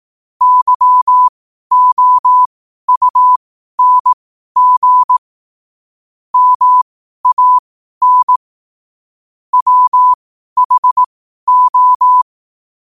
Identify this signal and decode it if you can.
Morse code, heard as a keyed tone, about 18 wpm, spaced as for 12 wpm, 1000 Hz. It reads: YOUNG MAN WHO
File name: qotdfiles / 12wpm